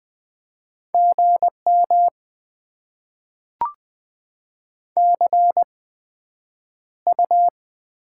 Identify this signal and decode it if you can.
Morse code, heard as a keyed tone, about 20 wpm, spaced as for 9 wpm, 700 Hz. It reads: GM CU
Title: Top 20 QSO Elements - 8x Character Spacing - 20wpm